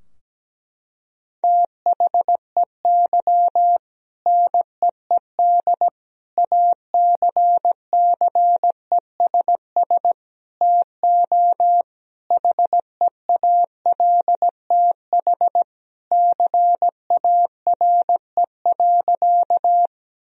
Morse code, heard as a keyed tone, about 17 wpm, 700 Hz. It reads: THEY NEED ACCESS TO HEALTH CARE.
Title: Sentences from Top 1000 Words - Review Part 4 - 17wpm